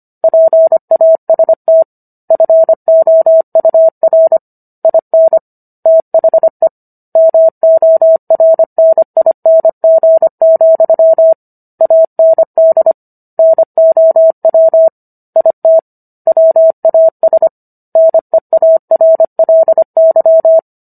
Morse code, heard as a keyed tone, about 25 wpm, 650 Hz. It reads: PAST FOUR IN THE MORNING, AND NOW IT WAS NEARLY
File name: Book One / 16-Chapter 16 - 25 WPM